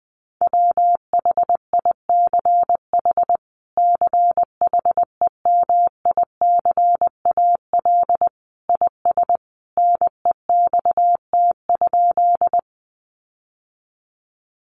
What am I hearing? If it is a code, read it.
Morse code, heard as a keyed tone, about 20 wpm, 700 Hz. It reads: WHICH CHEMICAL IS NEXT?
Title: Sentences from Top 1200 Words 20wpm